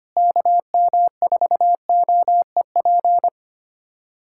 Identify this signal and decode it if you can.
Morse code, heard as a keyed tone, about 25 wpm, 700 Hz. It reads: KM4OEP
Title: USA Callsign DE Callsign - 6 Characters - 25wpm